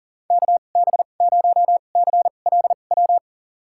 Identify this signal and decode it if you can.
Morse code, heard as a keyed tone, about 40 wpm, spaced as for 29 wpm, 700 Hz. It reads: KB0CLW
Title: USA Call Signs - 2x Character Spacing - 40wpm